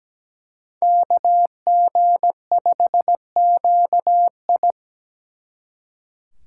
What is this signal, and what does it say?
Morse code, heard as a keyed tone, about 20 wpm, 700 Hz. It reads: KG5QI
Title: Speed Racing - USA 2x2 Call Signs 17wpm